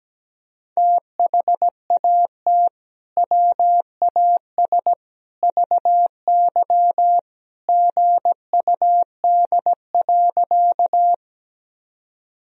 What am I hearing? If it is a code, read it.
Morse code, heard as a keyed tone, about 17 wpm, 700 Hz. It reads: THAT WAS VY GUD.
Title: Sentences from Top 200 Words and Abbreviations 17wpm